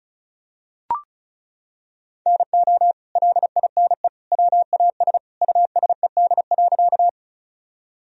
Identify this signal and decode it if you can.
Morse code, heard as a keyed tone, about 34 wpm, 700 Hz. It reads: NO LINE WAS USED.
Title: Sentences from Top 1000 Words - Review Part 1 - 35wpm